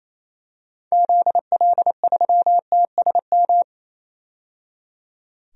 Morse code, heard as a keyed tone, about 28 wpm, 700 Hz. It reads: ZL3TSM
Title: DXCC-100 Call Signs - 6 Characters - 28wpm